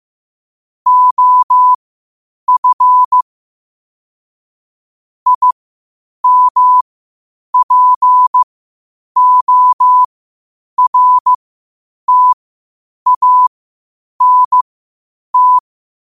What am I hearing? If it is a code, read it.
Morse code, heard as a keyed tone, about 15 wpm, spaced as for 8 wpm, 1000 Hz. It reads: OF IMPORTANT
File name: qotdfiles / 8wpm